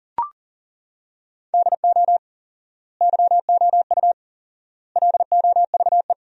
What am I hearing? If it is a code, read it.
Morse code, heard as a keyed tone, about 40 wpm, 700 Hz. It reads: DO YOU LOVE
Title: 4x Word-Spacing - Sentences from Top 400 Words - 40wpm